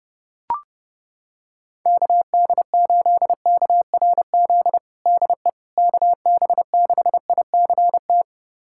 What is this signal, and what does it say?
Morse code, heard as a keyed tone, about 29 wpm, 700 Hz. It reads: KD8KRZ DE KB6ICT
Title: USA Callsign DE Callsign 30wpm